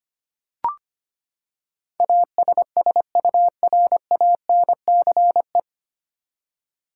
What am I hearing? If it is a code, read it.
Morse code, heard as a keyed tone, about 25 wpm, 700 Hz. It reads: ASSURANCE
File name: Nine Letter Words - 25wpm